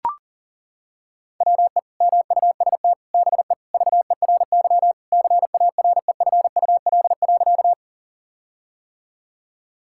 Morse code, heard as a keyed tone, about 37 wpm, 700 Hz. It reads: WE MUST BE VERY CAREFUL.
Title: Sentences from Top 1600 Words 40wpm